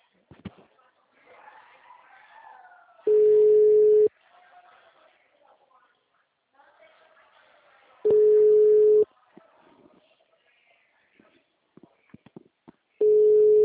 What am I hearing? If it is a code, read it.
Ringback tone (Europe, 425 Hz), with speech.
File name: out- -2007- - - .16329